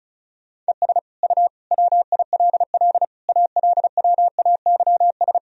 Morse code, heard as a keyed tone, about 36 wpm, 700 Hz. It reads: ES U WILL ALWAYS
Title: Sentences from Top 400 Words and Abbreviations 35wpm